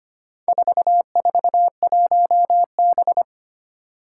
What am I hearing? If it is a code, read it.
Morse code, heard as a keyed tone, about 25 wpm, 700 Hz. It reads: 441B
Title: Four Challenging Letter-Number Combinations 25wpm